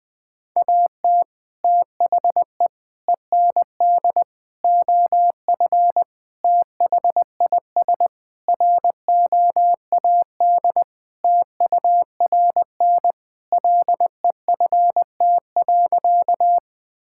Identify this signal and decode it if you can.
Morse code, heard as a keyed tone, about 20 wpm, 700 Hz. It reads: AT THE END OF THIS ROAD TURN LEFT.
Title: Sentences with 5-Word Phrases - 20wpm